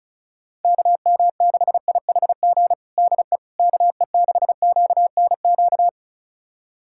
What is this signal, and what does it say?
Morse code, heard as a keyed tone, about 34 wpm, 700 Hz. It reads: KM6IHG DE KE6QNQ
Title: USA Callsign DE Callsign - 6 Characters - 35wpm